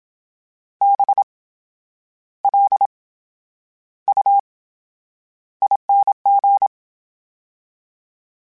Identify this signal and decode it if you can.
Morse code, heard as a keyed tone, about 27 wpm, spaced as for 9 wpm, 800 Hz. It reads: BLUING